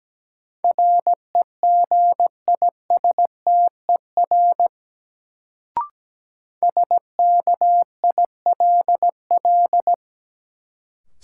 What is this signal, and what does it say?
Morse code, heard as a keyed tone, about 17 wpm, 700 Hz. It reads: REGISTER SKILL
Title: Sentences from Top 1200 Words - New Word Drill Alt - 17wpm